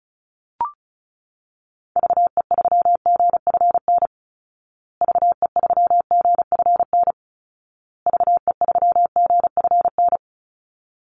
Morse code, heard as a keyed tone, about 34 wpm, 700 Hz. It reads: VE3GFN VE3GFN VE3GFN
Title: Call Signs Repeated 2x - 35wpm